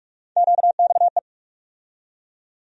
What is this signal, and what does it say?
Morse code, heard as a keyed tone, about 43 wpm, 700 Hz. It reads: QXE
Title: USA Call Sign Postfixes - 3 Letters 45wpm